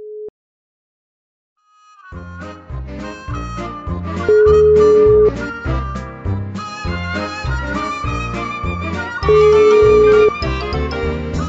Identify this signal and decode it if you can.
Ringback tone (Europe, 425 Hz).